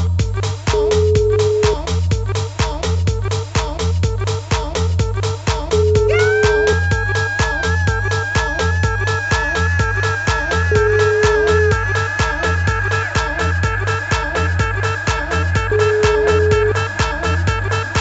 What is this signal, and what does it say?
Ringback tone (Europe, 425 Hz), with noise.